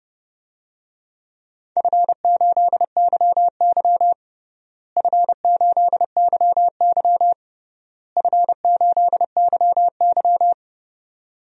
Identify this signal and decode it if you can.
Morse code, heard as a keyed tone, about 29 wpm, 700 Hz. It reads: F8YY F8YY F8YY
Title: DXCC-100 Call Signs - 4 Characters - Repeated 3x - 30wpm